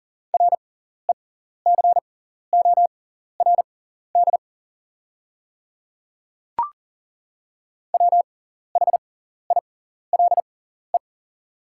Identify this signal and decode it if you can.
Morse code, heard as a keyed tone, about 40 wpm, spaced as for 14 wpm, 690 Hz. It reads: RECORD WHILE
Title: 6x Character Spacing - Top 500 Words - 40wpm